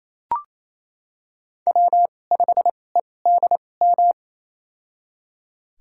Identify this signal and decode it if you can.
Morse code, heard as a keyed tone, about 28 wpm, spaced as for 20 wpm, 700 Hz. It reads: W5EDM
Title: USA Call Signs - 5 Characters - 2x Character Spacing - 28wpm